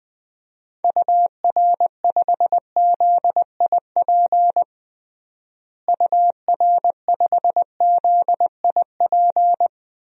Morse code, heard as a keyed tone, about 20 wpm, 700 Hz. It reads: UR5ZIP UR5ZIP
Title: DXCC-100 Call Signs - 6 Characters - Repeated 3x - 20wpm